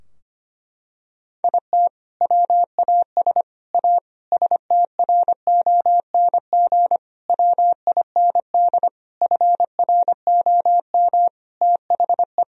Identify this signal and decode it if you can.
Morse code, heard as a keyed tone, about 25 wpm, 700 Hz. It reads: IT WAS A STRONG WIND FROM THE
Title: Sentences from Top 1000 Words - Review Part 2 - 25wpm